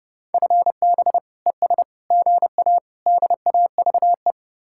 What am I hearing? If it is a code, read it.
Morse code, heard as a keyed tone, about 30 wpm, 700 Hz. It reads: FB ES GA DAVE